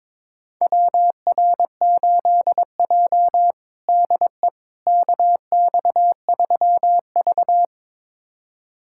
Morse code, heard as a keyed tone, about 21 wpm, 700 Hz. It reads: WR8J DE KX3V
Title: USA Callsign DE Callsign - 4 Characters - 22wpm